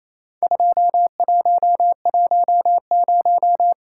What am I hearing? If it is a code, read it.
Morse code, heard as a keyed tone, about 28 wpm, 700 Hz. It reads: 2110
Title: Worked All Europe DX Contest - QTC Exchange - 28wpm